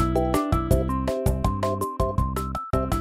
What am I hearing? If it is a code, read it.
There is no code.